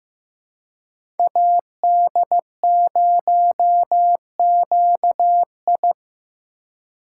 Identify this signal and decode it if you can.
Morse code, heard as a keyed tone, about 15 wpm, 700 Hz. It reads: AD0QI
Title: USA Call Signs - 5 Characters - 2x Character Spacing - 15wpm